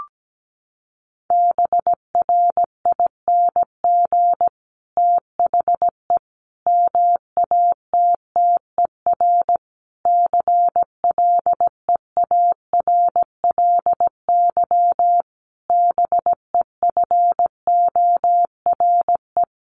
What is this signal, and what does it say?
Morse code, heard as a keyed tone, about 17 wpm, 700 Hz. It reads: BRING THE MATTER CLEARLY BEFORE
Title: Sentences from Top 1300 Words 17wpm